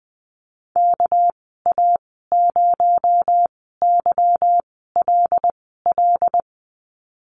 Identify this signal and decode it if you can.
Morse code, heard as a keyed tone, about 20 wpm, spaced as for 15 wpm, 700 Hz. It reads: KA0YLL
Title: 219 Lesson 35g - Number 0 - Call Signs 20wpm